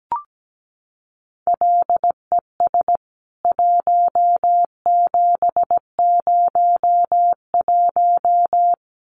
Morse code, heard as a keyed tone, about 17 wpm, 700 Hz. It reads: LES 1701
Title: CWT Exchange 17wpm